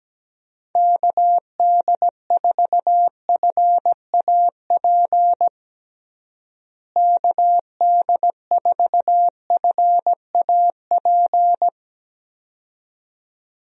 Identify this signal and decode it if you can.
Morse code, heard as a keyed tone, about 17 wpm, 700 Hz. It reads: KD4FAP KD4FAP
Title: USA Call Signs - 6 Characters - Repeated 2x - 17wpm